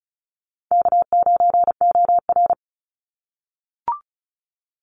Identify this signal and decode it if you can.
Morse code, heard as a keyed tone, about 34 wpm, 700 Hz. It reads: K9OR
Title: Call Signs - 35wpm